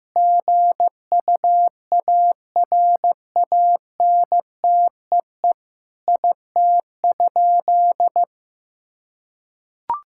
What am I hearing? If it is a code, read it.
Morse code, heard as a keyed tone, about 15 wpm, 700 Hz. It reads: GUARANTEE IT?
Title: Sentences from Top 1600 Words 15wpm